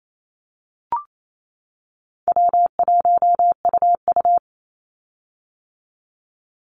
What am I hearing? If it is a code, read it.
Morse code, heard as a keyed tone, about 28 wpm, 700 Hz. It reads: W1UU
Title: Easy Call Signs - 28wpm